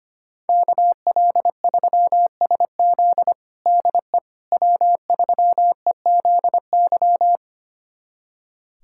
Morse code, heard as a keyed tone, about 25 wpm, 700 Hz. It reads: KL3SZ DE W3EZY
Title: USA Callsign DE Callsign - 5 Characters - 25wpm